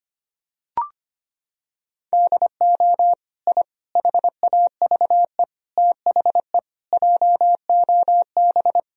Morse code, heard as a keyed tone, about 25 wpm, 700 Hz. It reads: DO I HAVE THE JOB
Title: Sentences from Top 1000 Words - Review Part 3 - 25wpm